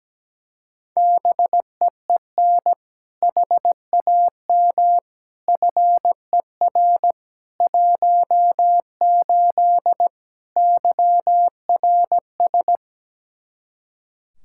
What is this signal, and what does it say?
Morse code, heard as a keyed tone, about 17 wpm, 700 Hz. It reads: BEEN HAM FER 18 YRS
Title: Practice QSO - Exchange 3 - More Info - 17wpm